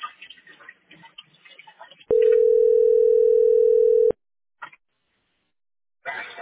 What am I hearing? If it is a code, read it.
Ringback tone (North America), 440 + 480 Hz.